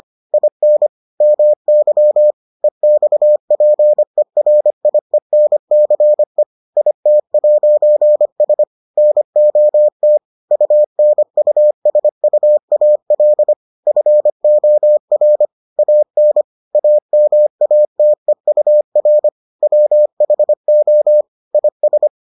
Morse code, heard as a keyed tone, about 25 wpm, 600 Hz. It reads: IN MY EXPERIENCE IT'S NOT UNUSUAL FOR AN AMATEUR WHO IS